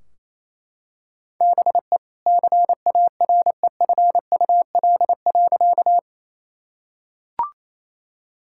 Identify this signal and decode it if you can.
Morse code, heard as a keyed tone, about 28 wpm, 700 Hz. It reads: BE CAREFUL.
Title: Sentences from Top 1600 Words 28wpm